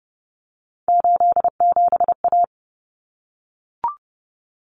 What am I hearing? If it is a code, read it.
Morse code, heard as a keyed tone, about 30 wpm, 700 Hz. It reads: 87A
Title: Common Cut Numbers - 3-Digit 30wpm